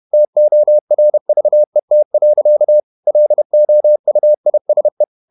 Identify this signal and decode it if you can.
Morse code, heard as a keyed tone, about 31 wpm, 600 Hz. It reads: TORVET. LOUISE
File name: Ferie QRQ 31wpm